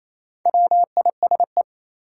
Morse code, heard as a keyed tone, about 28 wpm, 700 Hz. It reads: WISE
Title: Sentences from Top 1600 Words - New Word Drill - 28wpm